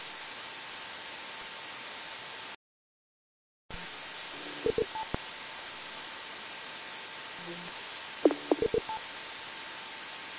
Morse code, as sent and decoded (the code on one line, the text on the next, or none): none